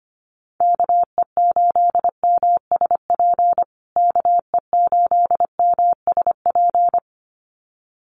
Morse code, heard as a keyed tone, about 25 wpm, 700 Hz. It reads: KE8MSP KE8MSP
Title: USA Call Signs Repeated 2x 25wpm